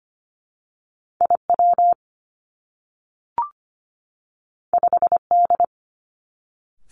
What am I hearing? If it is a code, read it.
Morse code, heard as a keyed tone, about 25 wpm, 700 Hz. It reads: I W 5 D
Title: Two Challenging Letter-Number Combinations 25wpm